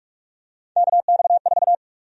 Morse code, heard as a keyed tone, about 46 wpm, 700 Hz. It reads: KX4
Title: USA Call Sign Prefixes - 45wpm